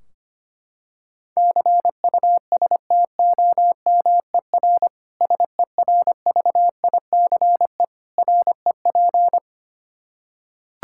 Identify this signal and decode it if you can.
Morse code, heard as a keyed tone, about 25 wpm, 700 Hz. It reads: CUSTOMER SERVICE REP
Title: Practice QSO - Exchange 3 - More Info - 25wpm